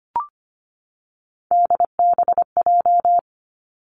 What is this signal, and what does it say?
Morse code, heard as a keyed tone, about 25 wpm, 700 Hz. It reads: DBJ
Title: Three Challenging Letter-Number Combinations 25wpm